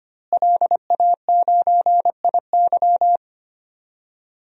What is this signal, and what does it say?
Morse code, heard as a keyed tone, about 25 wpm, 700 Hz. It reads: LA9IY
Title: DXCC-100 Call Signs - 5 Characters - Repeated 2x - 25wpm